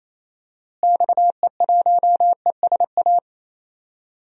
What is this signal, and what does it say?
Morse code, heard as a keyed tone, about 28 wpm, 700 Hz. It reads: XE1ESA
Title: DXCC-100 Call Signs - 6 Characters - Repeated 2x - 28wpm